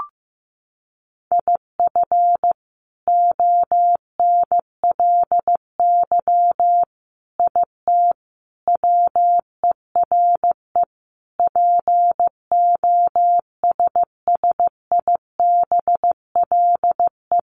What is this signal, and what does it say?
Morse code, heard as a keyed tone, about 15 wpm, 700 Hz. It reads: IF ONLY IT WERE POSSIBLE
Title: Sentences from Top 500 Words 15wpm